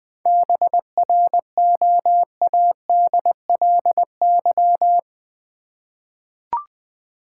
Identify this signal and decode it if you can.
Morse code, heard as a keyed tone, about 20 wpm, 700 Hz. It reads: BROADLY
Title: 7 Letter Words - First 3 Letters Given - 20wpm